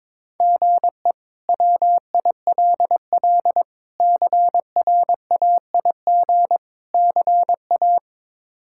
Morse code, heard as a keyed tone, about 22 wpm, 700 Hz. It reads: GE WILL CRAIG CA
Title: K1USN SST QSO Practice - 22wpm